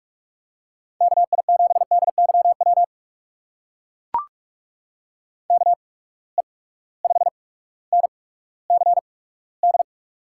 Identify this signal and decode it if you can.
Morse code, heard as a keyed tone, about 44 wpm, spaced as for 19 wpm, 700 Hz. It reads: KI7DYW KE5NCD
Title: USA Call Signs - 6 Characters - 8x Character Spacing - 45wpm